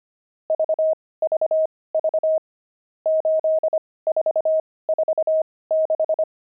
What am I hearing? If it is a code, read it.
Morse code, heard as a keyed tone, about 25 wpm, spaced as for 18 wpm, 630 Hz. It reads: VVV 8446
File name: pangram-fs-18-cs-25-nums14